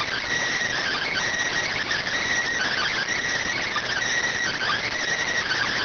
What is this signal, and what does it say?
Emergency Alert System (SAME) tones, heard as harsh buzzing bursts. It is heard from 0.8 to 5.0 s.